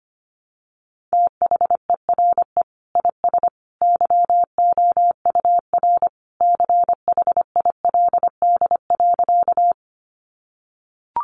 Morse code, heard as a keyed tone, about 25 wpm, 700 Hz. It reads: THERE IS YOUR CHILD.
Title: Sentences from Top 800 Words 25wpm